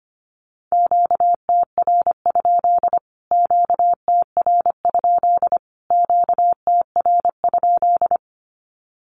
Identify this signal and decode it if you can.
Morse code, heard as a keyed tone, about 25 wpm, 700 Hz. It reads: QTR? QTR? QTR?
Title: Q-Codes Repeated 3x 25wpm